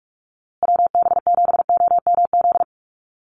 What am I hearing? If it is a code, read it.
Morse code, heard as a keyed tone, about 44 wpm, 700 Hz. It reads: WB7OMZ
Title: USA Call Signs Repeated 2x 45wpm